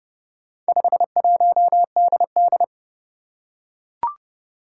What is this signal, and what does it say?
Morse code, heard as a keyed tone, about 30 wpm, 700 Hz. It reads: 51DD
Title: Four Challenging Letter-Number Combinations 30wpm